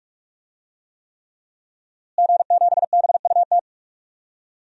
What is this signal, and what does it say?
Morse code, heard as a keyed tone, about 46 wpm, 700 Hz. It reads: G7BUT
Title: DXCC-100 Call Signs - 5 Characters - Repeated 3x - 45wpm